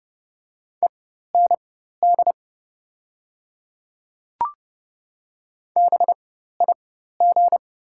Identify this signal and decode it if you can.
Morse code, heard as a keyed tone, about 29 wpm, spaced as for 14 wpm, 700 Hz. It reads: END BIG
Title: 4x Character Spacing - Top 300 Words - 30wpm